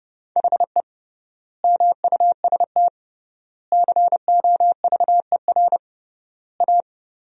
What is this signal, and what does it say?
Morse code, heard as a keyed tone, about 31 wpm, 700 Hz. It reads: HE MUST COVER A
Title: 3x Word-Spacing - Sentences from Top 500 Words - 30wpm